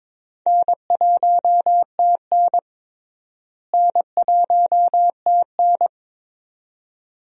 Morse code, heard as a keyed tone, about 22 wpm, 700 Hz. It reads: N1TN N1TN
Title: USA Call Signs - 4 Characters - Repeated 2x - 22wpm